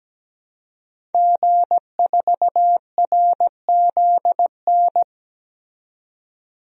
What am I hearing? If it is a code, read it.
Morse code, heard as a keyed tone, about 17 wpm, 700 Hz. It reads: G4RZN
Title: DXCC-100 Call Signs - 5 Characters - 17wpm